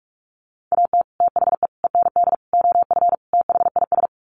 Morse code, heard as a keyed tone, about 44 wpm, 700 Hz. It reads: AT THE END OF THIS
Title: Sets of 5 Words 45wpm